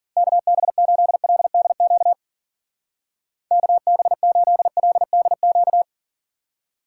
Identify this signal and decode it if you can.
Morse code, heard as a keyed tone, about 50 wpm, 700 Hz. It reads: KB8LDQ KB8LDQ